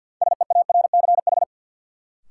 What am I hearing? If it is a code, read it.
Morse code, heard as a keyed tone, about 53 wpm, 700 Hz. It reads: SEARCH
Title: Sentences from Top 800 Words - New Word Drill - 50wpm